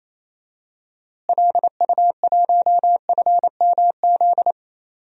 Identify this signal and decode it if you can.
Morse code, heard as a keyed tone, about 28 wpm, 700 Hz. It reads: LU1FMZ
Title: DXCC-100 Call Signs - 6 Characters - Repeated 3x - 28wpm